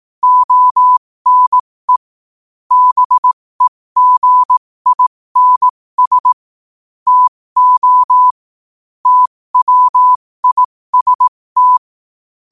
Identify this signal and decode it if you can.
Morse code, heard as a keyed tone, about 18 wpm, spaced as for 15 wpm, 1000 Hz. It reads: ONE BEGINS TO TWIST
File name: qotdfiles / 15wpm